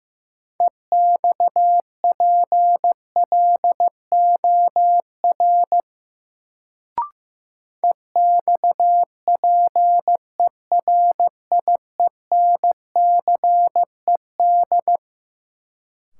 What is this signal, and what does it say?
Morse code, heard as a keyed tone, about 15 wpm, 700 Hz. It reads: EXPLOR EXPERIENCED
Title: Sentences from Top 1100 Words - New Word Drill - 15wpm